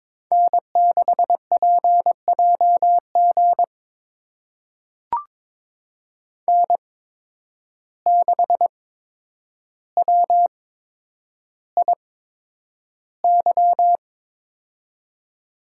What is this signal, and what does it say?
Morse code, heard as a keyed tone, about 22 wpm, spaced as for 9 wpm, 700 Hz. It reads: N6PJG N6WIY
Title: USA Call Signs - 5 Characters - 8x Character Spacing - 22wpm